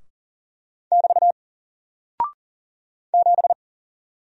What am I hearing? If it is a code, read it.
Morse code, heard as a keyed tone, about 40 wpm, 700 Hz. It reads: <BT> 7
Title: Single Character (Letter:Number:Punctuation:Prosign) Alt 40wpm